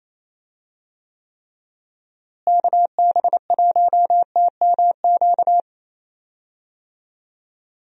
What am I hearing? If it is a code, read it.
Morse code, heard as a keyed tone, about 28 wpm, 700 Hz. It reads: KB1TMQ
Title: USA Call Signs - 6 Characters - Repeated 2x - 28wpm